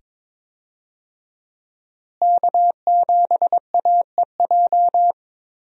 Morse code, heard as a keyed tone, about 22 wpm, 700 Hz. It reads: K7AEJ
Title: USA Call Signs - 5 Characters - Repeated 2x - 22wpm